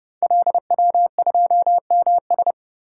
Morse code, heard as a keyed tone, about 30 wpm, 700 Hz. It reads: LW2MS
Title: Code Groups - 5 Letter:Numbers - 30wpm